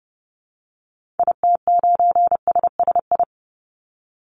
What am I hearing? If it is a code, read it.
Morse code, heard as a keyed tone, about 30 wpm, 700 Hz. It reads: IT9SSI
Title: Call Signs Repeated 3x - 30wpm